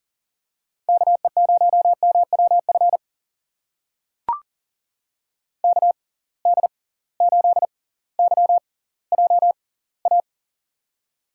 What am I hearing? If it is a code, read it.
Morse code, heard as a keyed tone, about 37 wpm, spaced as for 20 wpm, 700 Hz. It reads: KE0MWF KD8YJA
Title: USA Call Signs - 6 Characters - 6x Character Spacing - 40wpm